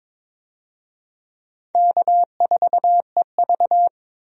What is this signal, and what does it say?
Morse code, heard as a keyed tone, about 22 wpm, 700 Hz. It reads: K4EV
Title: USA Call Signs - 4 Characters - 22wpm